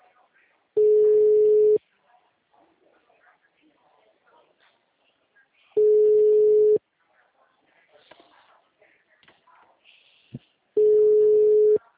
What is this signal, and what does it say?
Ringback tone (Europe, 425 Hz), with speech.